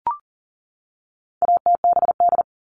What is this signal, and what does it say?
Morse code, heard as a keyed tone, about 40 wpm, 700 Hz. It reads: ATBD
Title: All Cut Numbers - 4-Digit 40wpm